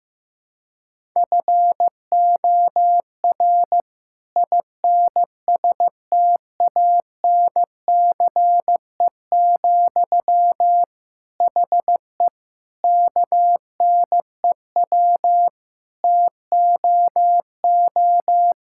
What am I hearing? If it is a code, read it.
Morse code, heard as a keyed tone, about 15 wpm, 690 Hz. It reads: FOR INSTANCE, HE KNEW TOO
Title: Sentences from Top 1000 Words - Review Part 5 - 15wpm